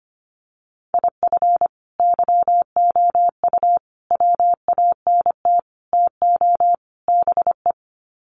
Morse code, heard as a keyed tone, about 25 wpm, 700 Hz. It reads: IF YOU WANT TO BE
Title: Sets of 5 Words Encore 25wpm